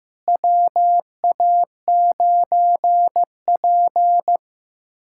Morse code, heard as a keyed tone, about 15 wpm, 700 Hz. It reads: WA9P
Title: USA Callsign DE Callsign - 4 Characters - 15wpm